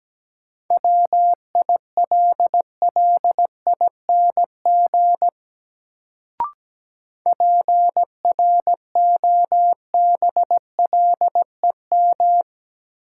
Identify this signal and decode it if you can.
Morse code, heard as a keyed tone, about 17 wpm, 700 Hz. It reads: WILLING PROBLEM
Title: Sentences from Top 1000 Words - New Word Drill - 17wpm